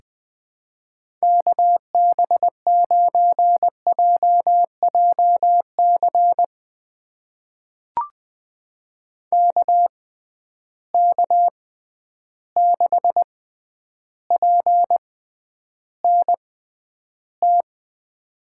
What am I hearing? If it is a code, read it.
Morse code, heard as a keyed tone, about 20 wpm, spaced as for 10 wpm, 700 Hz. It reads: KB9JJC KK6PNT
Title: USA Call Signs - 6 Characters - 6x Character Spacing - 20wpm